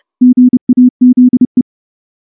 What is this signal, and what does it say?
Morse code, heard as a keyed tone, about 30 wpm, 250 Hz. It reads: GAZE
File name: morse-training-30WPM